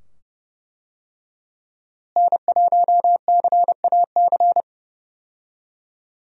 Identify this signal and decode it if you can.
Morse code, heard as a keyed tone, about 30 wpm, 700 Hz. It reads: N1CAC
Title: USA Call Signs - 5 Characters - Repeated 2x - 30wpm